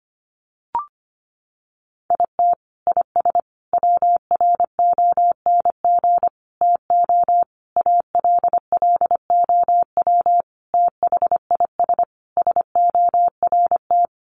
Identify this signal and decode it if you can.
Morse code, heard as a keyed tone, about 25 wpm, 700 Hz. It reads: IT IS WRONG TO ALLOW THIS SORT
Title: Sentences from Top 1300 Words 25wpm